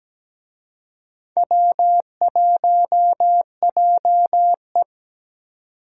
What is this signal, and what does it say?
Morse code, heard as a keyed tone, about 20 wpm, 700 Hz. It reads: W1JE
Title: Speed Racing - USA 1x2 Call Signs 17wpm